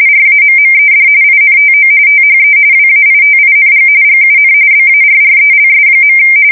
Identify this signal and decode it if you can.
RTTY, about 45.45 baud. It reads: RAMMED INTO THE WAITING GUN BARRELS.
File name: fsk170 / USS-New-Jersey